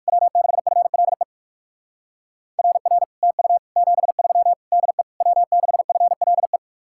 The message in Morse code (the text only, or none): W6FLE RR TU 73 DE W6FLE